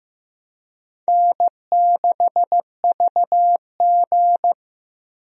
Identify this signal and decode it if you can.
Morse code, heard as a keyed tone, about 15 wpm, 700 Hz. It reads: N6VG
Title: USA Call Signs - 4 Characters - Repeated 3x - 15wpm